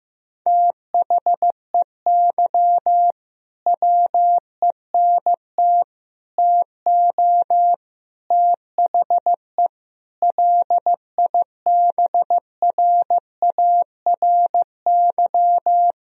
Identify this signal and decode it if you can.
Morse code, heard as a keyed tone, about 15 wpm, 700 Hz. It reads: THEY WENT TO THE LIBRARY
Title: Sentences from Top 1000 Words - Review Part 4 - 15wpm